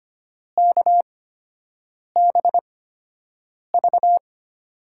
Morse code, heard as a keyed tone, about 25 wpm, spaced as for 7 wpm, 700 Hz. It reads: KBV
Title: USA Call Signs - 8x Character Spacing - 25wpm